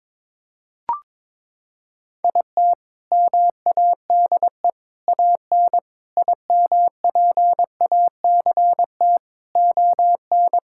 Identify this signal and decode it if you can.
Morse code, heard as a keyed tone, about 22 wpm, 700 Hz. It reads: IT MADE AN IMPACT ON
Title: Sentences from Top 1000 Words - Review Part 6 - 22wpm